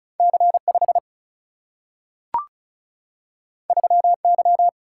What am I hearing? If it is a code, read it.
Morse code, heard as a keyed tone, about 36 wpm, 700 Hz. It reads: C 5 3 Y
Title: Single-Letter Single-Number 35wpm